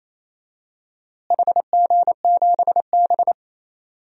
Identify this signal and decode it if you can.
Morse code, heard as a keyed tone, about 28 wpm, 700 Hz. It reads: HG7B
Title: DXCC-100 Call Signs - 4 Characters - Repeated 2x - 28wpm